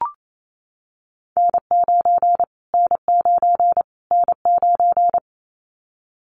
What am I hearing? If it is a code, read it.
Morse code, heard as a keyed tone, about 28 wpm, 700 Hz. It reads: N9 N9 N9
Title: Callsign Prefixes for Top 25 DXCC - Repeated 3x - 28wpm